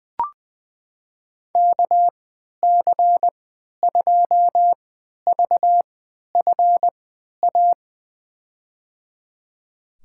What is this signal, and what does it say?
Morse code, heard as a keyed tone, about 20 wpm, spaced as for 11 wpm, 700 Hz. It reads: KC2VFA
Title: USA Call Signs - 6 Characters - 3x Character Spacing - 20wpm